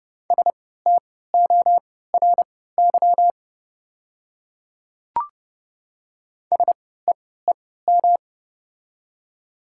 Morse code, heard as a keyed tone, about 30 wpm, spaced as for 17 wpm, 700 Hz. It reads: STORY SEEM
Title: 3x Character Spacing - Top 300 Words - 30wpm